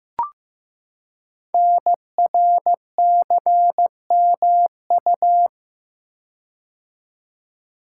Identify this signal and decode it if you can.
Morse code, heard as a keyed tone, about 15 wpm, 700 Hz. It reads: NRCMU
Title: Code Groups - 5 Letters - 15wpm